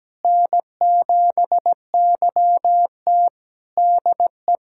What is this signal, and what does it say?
Morse code, heard as a keyed tone, about 17 wpm, 700 Hz. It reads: N7YT DE
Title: Practice QSO - Exchange 4 - Ending - 17wpm